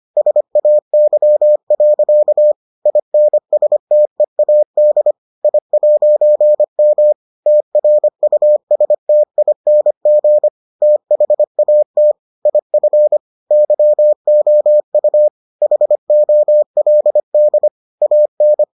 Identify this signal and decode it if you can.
Morse code, heard as a keyed tone, about 25 wpm, 600 Hz. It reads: SAY. INSTEAD I'M TRUSTING THAT IF YOU HOLD AN